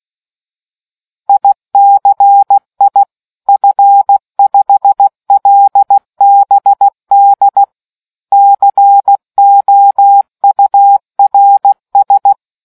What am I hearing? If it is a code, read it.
Morse code, heard as a keyed tone, about 16 wpm, 800 Hz. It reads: ICI F5LBD COURS